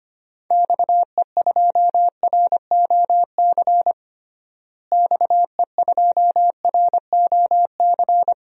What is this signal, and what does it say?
Morse code, heard as a keyed tone, about 25 wpm, 700 Hz. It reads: XE2ROC XE2ROC
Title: DXCC-100 Call Signs - 6 Characters - Repeated 2x - 25wpm